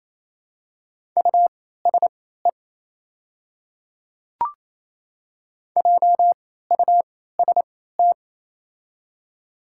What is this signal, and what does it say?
Morse code, heard as a keyed tone, about 28 wpm, spaced as for 16 wpm, 700 Hz. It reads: USE JUST
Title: 3x Character Spacing - Top 200 Words - 28wpm